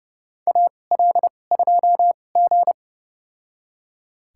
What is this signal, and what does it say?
Morse code, heard as a keyed tone, about 30 wpm, spaced as for 22 wpm, 700 Hz. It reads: AL2G
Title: USA Call Signs - 4 Characters - 2x Character Spacing - 30wpm